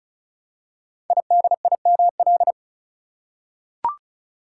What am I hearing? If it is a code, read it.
Morse code, heard as a keyed tone, about 35 wpm, 700 Hz. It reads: IDIML
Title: Code Groups - 5 Letter:Numbers - 35wpm